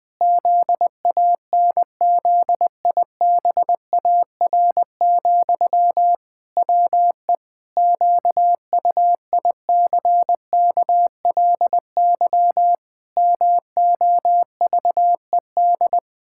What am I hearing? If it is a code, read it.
Morse code, heard as a keyed tone, about 20 wpm, 700 Hz. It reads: ZANZIBAR, WE QUICKLY MOVED
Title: Pangram Challenge v2 - 20wpm